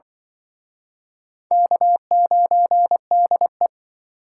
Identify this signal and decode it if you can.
Morse code, heard as a keyed tone, about 25 wpm, 700 Hz. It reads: K9DE